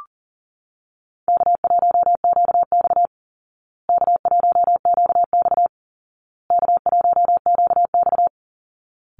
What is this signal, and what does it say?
Morse code, heard as a keyed tone, about 40 wpm, 700 Hz. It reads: K1QX K1QX K1QX
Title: Call Signs Repeated 2x - 40wpm